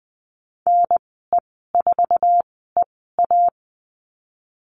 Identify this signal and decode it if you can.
Morse code, heard as a keyed tone, about 20 wpm, spaced as for 14 wpm, 700 Hz. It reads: NE4EA
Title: 048 Lesson 9g - Number 4 - Call Signs 20wpm